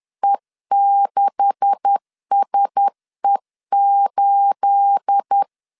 Morse code, heard as a keyed tone, about 10 wpm, 800 Hz. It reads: E6SE8
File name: LEZIONI / 9 286 10wpm 2735